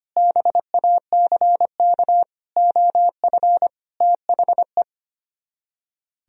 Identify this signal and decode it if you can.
Morse code, heard as a keyed tone, about 30 wpm, 700 Hz. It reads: BACK OF THE
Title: Speed-Racing - Sets of 5 Words 25wpm